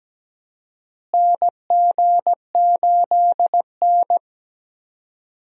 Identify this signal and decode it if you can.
Morse code, heard as a keyed tone, about 17 wpm, 700 Hz. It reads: NG8N
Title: USA Call Signs - 4 Characters - Repeated 2x - 17wpm